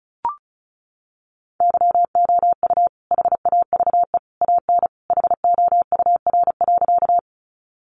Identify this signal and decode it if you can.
Morse code, heard as a keyed tone, about 34 wpm, 700 Hz. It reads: YOU HAVE AN HOUR.
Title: Sentences from Top 400 Words 35wpm